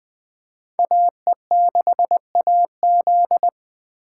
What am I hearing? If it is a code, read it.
Morse code, heard as a keyed tone, about 21 wpm, 700 Hz. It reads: AE6AZ